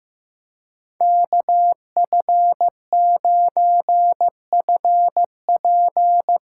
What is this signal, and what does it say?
Morse code, heard as a keyed tone, about 15 wpm, 700 Hz. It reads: KF9FP
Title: USA Call Signs - 4x Character Spacing - 15wpm